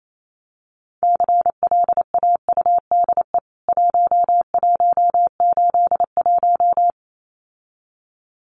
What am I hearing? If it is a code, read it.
Morse code, heard as a keyed tone, about 28 wpm, 700 Hz. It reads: CLAUDE 1181
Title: CWT Exchange 28wpm